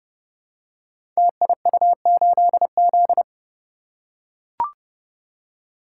Morse code, heard as a keyed tone, about 28 wpm, 700 Hz. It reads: TIU8Z
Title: Code Groups - 5 Letter:Numbers - 30wpm